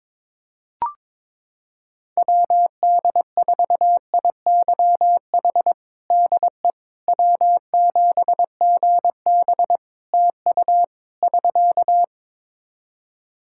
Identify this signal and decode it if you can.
Morse code, heard as a keyed tone, about 22 wpm, 700 Hz. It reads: WD4IYH DE W7GB TU <SK>
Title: Practice QSO - Exchange 4 - Ending - 22wpm